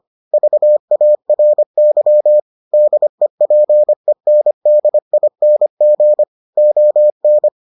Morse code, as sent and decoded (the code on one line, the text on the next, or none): ...- .- .-. -.-- / -.. .--. -. -.. .. -. --. / --- -.
VARY DEPENDING ON